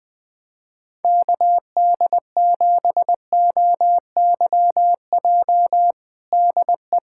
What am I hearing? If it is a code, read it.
Morse code, heard as a keyed tone, about 20 wpm, 700 Hz. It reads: KD7OYJ DE
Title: USA Callsign DE Callsign - 6 Characters - 20wpm